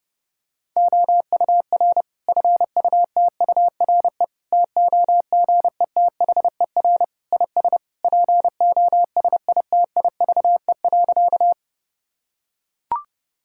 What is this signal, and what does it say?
Morse code, heard as a keyed tone, about 30 wpm, 700 Hz. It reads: OUR FUTURE TOGETHER IS POSITIVE.
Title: Sentences from Top 1000 Words - Review Part 5 - 30wpm